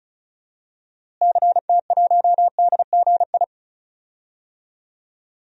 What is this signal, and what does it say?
Morse code, heard as a keyed tone, about 34 wpm, 700 Hz. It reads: CT1DGI
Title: DXCC-100 Call Signs - 6 Characters - Repeated 2x - 35wpm